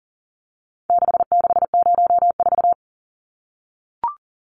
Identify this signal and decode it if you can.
Morse code, heard as a keyed tone, about 40 wpm, 700 Hz. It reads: 6604
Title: Numbers - 4-Digit 40wpm